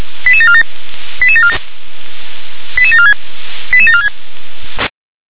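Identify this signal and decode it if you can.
Five-tone selective call (ZVEI), heard as quick runs of five tones, tones of about 70 ms. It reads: 88747 88747 88746 88746
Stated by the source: Five-tone ZVEI selective call 88082 received by SDR.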